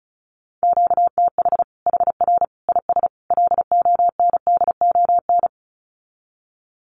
Morse code, heard as a keyed tone, about 34 wpm, 700 Hz. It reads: QTH HR IS LONDON
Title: Common QSO Phrases 35wpm